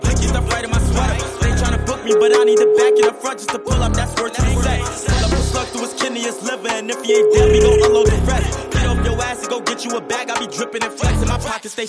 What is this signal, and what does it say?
Ringback tone (Europe, 425 Hz), with noise.